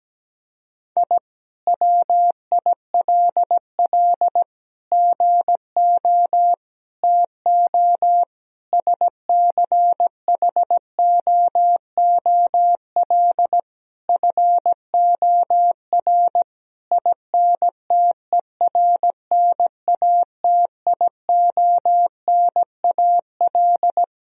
Morse code, heard as a keyed tone, about 17 wpm, 700 Hz. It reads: I WILL GO TO SCHOOL FOR INTERNATIONAL
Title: Sentences from Top 1000 Words - Review Part 4 - 17wpm